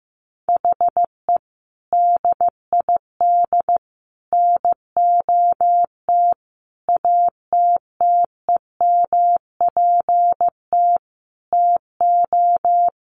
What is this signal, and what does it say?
Morse code, heard as a keyed tone, about 15 wpm, 700 Hz. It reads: HE DID NOT ATTEMPT TO
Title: Sentences from Top 1400 Words 15wpm